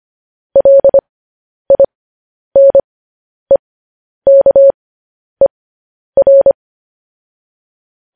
Morse code, heard as a keyed tone, about 25 wpm, spaced as for 10 wpm, 550 Hz. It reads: LINEKER
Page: News Headlines in Morse Code at 10 WPM